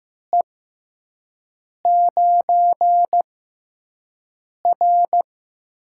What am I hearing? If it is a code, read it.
Morse code, heard as a keyed tone, about 15 wpm, spaced as for 5 wpm, 700 Hz. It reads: E9R
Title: USA Call Signs - 6x Character Spacing - 15wpm